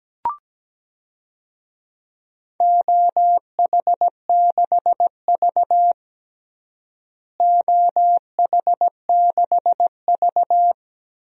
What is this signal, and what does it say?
Morse code, heard as a keyed tone, about 17 wpm, 700 Hz. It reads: OH6V OH6V
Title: DXCC-100 Call Signs - 4 Characters - Repeated 2x - 17wpm